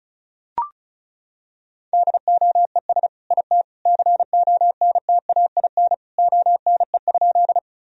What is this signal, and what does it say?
Morse code, heard as a keyed tone, about 34 wpm, 700 Hz. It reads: DOES IT CONTAIN ONE?
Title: Sentences from Top 1000 Words - Review Part 2 - 35wpm